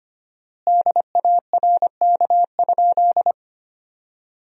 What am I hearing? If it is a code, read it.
Morse code, heard as a keyed tone, about 25 wpm, 700 Hz. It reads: DARK?
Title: 4x Word-Spacing - Sentences from Top 500 Words - 25wpm